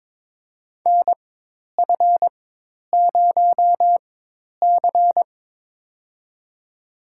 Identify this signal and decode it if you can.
Morse code, heard as a keyed tone, about 22 wpm, spaced as for 10 wpm, 700 Hz. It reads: NF0C
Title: USA Call Signs - 4 Characters - 4x Character Spacing - 22wpm